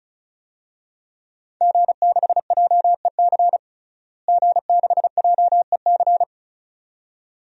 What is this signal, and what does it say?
Morse code, heard as a keyed tone, about 35 wpm, 700 Hz. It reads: G6JEC G6JEC
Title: DXCC-100 Call Signs - 5 Characters - Repeated 2x - 35wpm